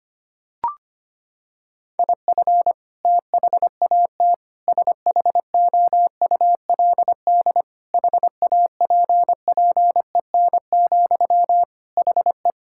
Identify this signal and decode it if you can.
Morse code, heard as a keyed tone, about 25 wpm, 700 Hz. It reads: IF THAT SHOULD HAPPEN, HE
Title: Sentences from Top 1000 Words - Review Part 6 - 25wpm